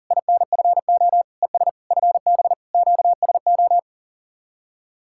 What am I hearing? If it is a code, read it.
Morse code, heard as a keyed tone, about 49 wpm, 700 Hz. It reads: INFO ES FB QSO
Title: Speed-Racing Common QSO Phrases - 40wpm